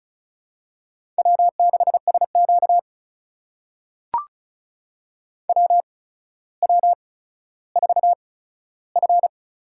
Morse code, heard as a keyed tone, about 36 wpm, spaced as for 15 wpm, 700 Hz. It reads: W6SQ WW4F
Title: USA Call Signs - 4 Characters - 8x Character Spacing - 35wpm